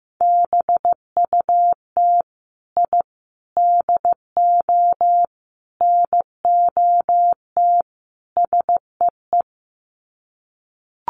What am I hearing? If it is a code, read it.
Morse code, heard as a keyed tone, about 14 wpm, 700 Hz. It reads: BUT I DO NOT SEE
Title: Sets of 5 Words Encore 15wpm